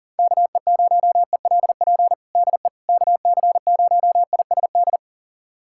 Morse code, heard as a keyed tone, about 40 wpm, 700 Hz. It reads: KE0ELP DE KC0ISD
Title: USA Callsign DE Callsign - 6 Characters - 40wpm